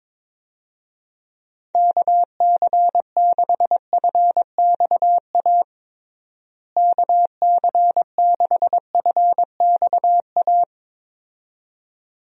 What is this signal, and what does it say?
Morse code, heard as a keyed tone, about 22 wpm, 700 Hz. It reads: KC6FXA KC6FXA
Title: USA Call Signs - 6 Characters - Repeated 2x - 22wpm